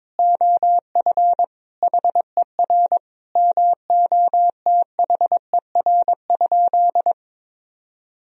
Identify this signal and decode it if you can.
Morse code, heard as a keyed tone, about 22 wpm, 700 Hz. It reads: OF HER MOTHER?
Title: Sentences from Top 1000 Words - Review Part 2 - 22wpm